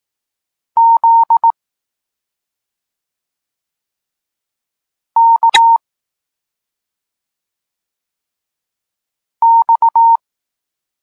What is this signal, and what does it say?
Morse code, heard as a keyed tone, about 18 wpm, 920 Hz. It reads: Z K X
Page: Tekst 32 – Doskonalenie odbioru poznanych liter